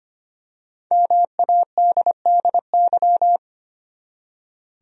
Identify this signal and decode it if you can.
Morse code, heard as a keyed tone, about 25 wpm, 700 Hz. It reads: MADDY
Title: USA Common Names - 25wpm